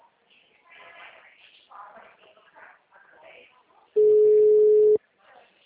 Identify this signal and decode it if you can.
Ringback tone (Europe, 425 Hz), with speech.